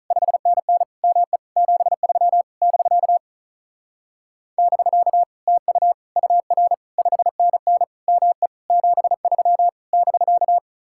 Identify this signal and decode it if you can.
Morse code, heard as a keyed tone, about 34 wpm, 700 Hz. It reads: 5NN ME 73 <BK> <BK> TU UR 5NN ME 73 <BK>
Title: Speed-Racing Common QSO Phrases - 35wpm